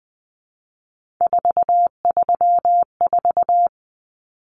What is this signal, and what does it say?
Morse code, heard as a keyed tone, about 20 wpm, 700 Hz. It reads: 434
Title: Common Cut Numbers - 3-Digit 20wpm